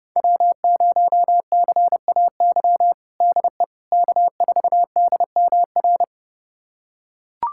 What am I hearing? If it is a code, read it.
Morse code, heard as a keyed tone, about 30 wpm, 700 Hz. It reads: W0CAY DE K4DMR
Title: USA Callsign DE Callsign - 5 Characters - 30wpm